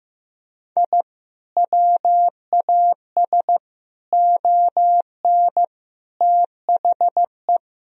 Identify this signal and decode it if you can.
Morse code, heard as a keyed tone, about 15 wpm, 700 Hz. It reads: I WAS ON THE
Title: Sentences from Top 1000 Words - Review Part 2 - 15wpm